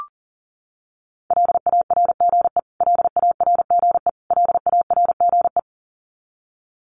Morse code, heard as a keyed tone, about 40 wpm, 700 Hz. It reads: LARGE LARGE LARGE
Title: Top 300 Words Repeated 40wpm